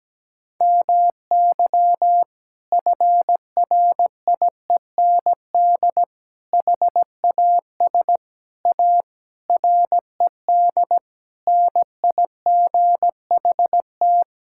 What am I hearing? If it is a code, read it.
Morse code, heard as a keyed tone, about 17 wpm, 700 Hz. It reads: MY FRIEND HAS A RED NIGHT
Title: Sentences from Top 1600 Words 17wpm